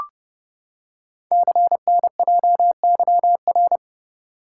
Morse code, heard as a keyed tone, about 29 wpm, 700 Hz. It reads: CNJYR
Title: Code Groups - 5 Letters - 30wpm